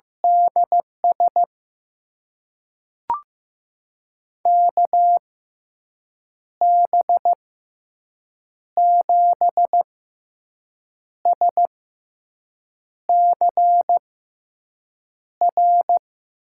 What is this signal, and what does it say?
Morse code, heard as a keyed tone, about 16 wpm, spaced as for 6 wpm, 700 Hz. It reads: DS KB7SCR
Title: USA Call Signs - 6 Characters - 6x Character Spacing - 15wpm